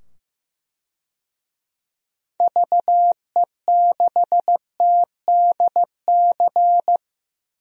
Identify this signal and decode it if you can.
Morse code, heard as a keyed tone, about 16 wpm, 700 Hz. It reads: VE6TDC
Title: DXCC-100 Call Signs - 6 Characters - 15wpm